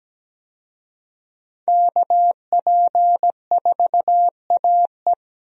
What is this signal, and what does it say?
Morse code, heard as a keyed tone, about 17 wpm, 700 Hz. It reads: KP4AE
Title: USA Call Signs - 5 Characters - Repeated 2x - 17wpm